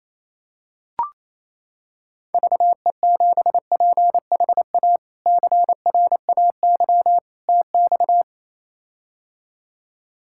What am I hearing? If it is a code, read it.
Morse code, heard as a keyed tone, about 28 wpm, 700 Hz. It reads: VE7PHA CRAY TX
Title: K1USN SST QSO Practice - 28wpm